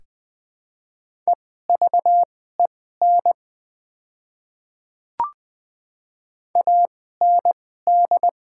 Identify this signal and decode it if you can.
Morse code, heard as a keyed tone, about 20 wpm, spaced as for 15 wpm, 700 Hz. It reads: EVEN AND
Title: 2x Character Spacing - Top 200 Words - 20wpm